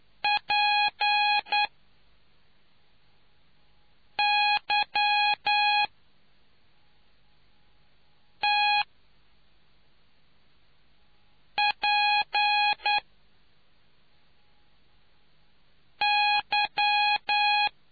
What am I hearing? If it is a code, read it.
Morse code, heard as a keyed tone, about 9 wpm, spaced as for 3 wpm, 830 Hz. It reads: PYTPY